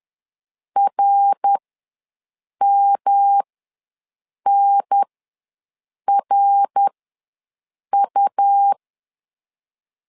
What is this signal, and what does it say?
Morse code, heard as a keyed tone, about 11 wpm, spaced as for 6 wpm, 800 Hz. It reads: RMNRU
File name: LEZIONI / 3 MWRO / 3 MWR ws 10wpm 2151